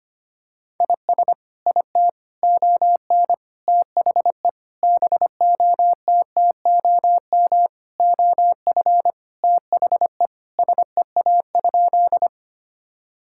Morse code, heard as a keyed tone, about 25 wpm, 700 Hz. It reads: IS IT ON THE BOTTOM OF THE SEA?
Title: Sentences from Top 1000 Words - Review Part 6 - 25wpm